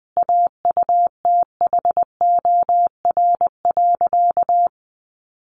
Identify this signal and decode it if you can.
Morse code, heard as a keyed tone, about 19 wpm, 700 Hz. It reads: AUTHOR.
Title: Sentences from Top 1200 Words 20wpm